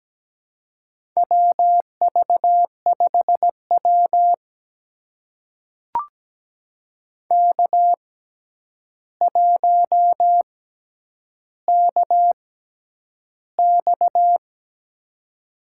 Morse code, heard as a keyed tone, about 17 wpm, spaced as for 9 wpm, 700 Hz. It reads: WV5W K1KX
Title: USA Call Signs - 4 Characters - 6x Character Spacing - 17wpm